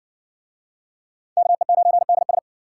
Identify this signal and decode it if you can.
Morse code, heard as a keyed tone, about 58 wpm, 700 Hz. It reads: KE9DS